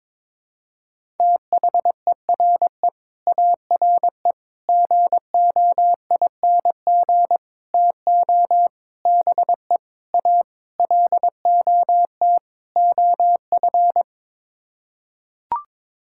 Morse code, heard as a keyed tone, about 21 wpm, 700 Hz. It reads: THERE ARE GOING TO BE A LOT OF
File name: practice / 22wpm / Sets of 8 Words - 22wpm